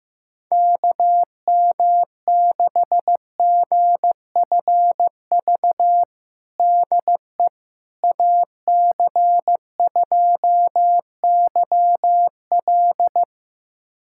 Morse code, heard as a keyed tone, about 15 wpm, 700 Hz. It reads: KM6GFV DE AC2YL
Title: USA Callsign DE Callsign 15wpm